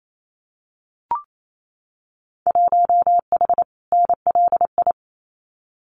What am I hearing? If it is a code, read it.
Morse code, heard as a keyed tone, about 28 wpm, 700 Hz. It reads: 1H NLI
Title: Winter Field Day Exchanges 28wpm